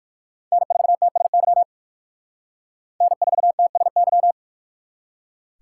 Morse code, heard as a keyed tone, about 55 wpm, 700 Hz. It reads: N4TSY N4TSY
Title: Speed Racing - USA 1x3 Call Signs 45wpm